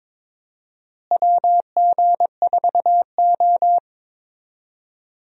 Morse code, heard as a keyed tone, about 21 wpm, 700 Hz. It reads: WG4O
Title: USA Call Signs - 4 Characters - Repeated 2x - 22wpm